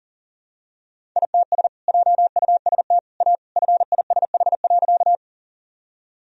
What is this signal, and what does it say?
Morse code, heard as a keyed tone, about 37 wpm, 700 Hz. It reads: ITS JUST A FISH.
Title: Sentences from Top 1000 Words - Review Part 1 - 40wpm